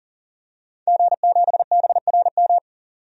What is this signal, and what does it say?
Morse code, heard as a keyed tone, about 40 wpm, 700 Hz. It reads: G7BRM
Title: DXCC-100 Call Signs - 5 Characters - 40wpm